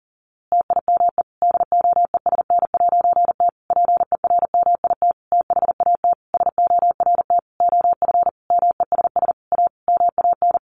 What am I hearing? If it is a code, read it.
Morse code, heard as a keyed tone, about 40 wpm, 700 Hz. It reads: TIME DOESN'T PERMIT THAT SORT OF MESS A MAN
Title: A Sound Of Thunder 40wpm